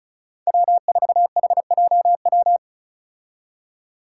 Morse code, heard as a keyed tone, about 34 wpm, 700 Hz. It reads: W4HJW
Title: Five Challenging Letter-Number Combinations 35wpm